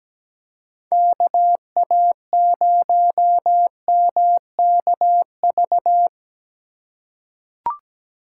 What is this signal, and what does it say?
Morse code, heard as a keyed tone, about 17 wpm, 700 Hz. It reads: KA0MKV
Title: USA Call Signs - 6 Characters - 3x Character Spacing - 17wpm